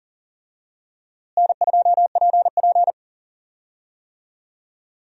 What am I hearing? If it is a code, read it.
Morse code, heard as a keyed tone, about 37 wpm, 700 Hz. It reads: N2PP
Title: USA Call Signs - 4 Characters - Repeated 2x - 40wpm